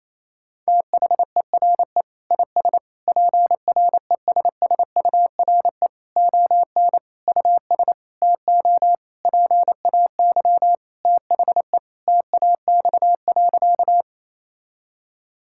Morse code, heard as a keyed tone, about 28 wpm, 700 Hz. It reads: THERE IS PRESSURE ON US TO PAY THE TAX.
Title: Sentences from Top 1000 Words - Review Part 6 - 28wpm